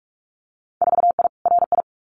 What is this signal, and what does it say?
Morse code, heard as a keyed tone, about 46 wpm, 700 Hz. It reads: 4I RI
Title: Winter Field Day Exchanges Encore 45wpm